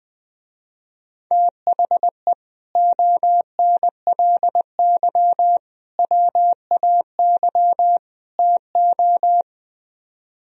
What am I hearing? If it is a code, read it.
Morse code, heard as a keyed tone, about 20 wpm, 700 Hz. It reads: THE ONLY WAY TO
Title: Speed-Racing - Sets of 4 Words 20wpm